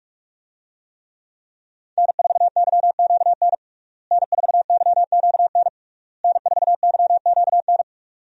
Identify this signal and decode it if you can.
Morse code, heard as a keyed tone, about 46 wpm, 700 Hz. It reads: N4YQN N4YQN N4YQN
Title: USA Call Signs - 5 Characters - Repeated 3x - 45wpm